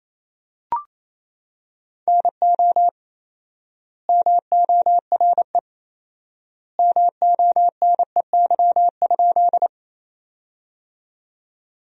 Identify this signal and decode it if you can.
Morse code, heard as a keyed tone, about 28 wpm, 700 Hz. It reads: NO MORE MONEY?
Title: 4x Word-Spacing - Sentences from Top 400 Words - 28wpm